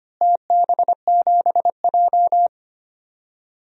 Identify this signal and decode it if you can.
Morse code, heard as a keyed tone, about 25 wpm, 700 Hz. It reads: TB7J
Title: DXCC-100 Call Signs - 4 Characters - Repeated 3x - 25wpm